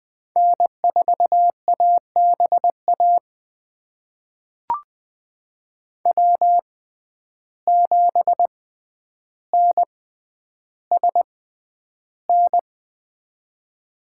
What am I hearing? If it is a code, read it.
Morse code, heard as a keyed tone, about 20 wpm, spaced as for 10 wpm, 700 Hz. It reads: N4ABA W7NSN
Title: USA Call Signs - 5 Characters - 6x Character Spacing - 20wpm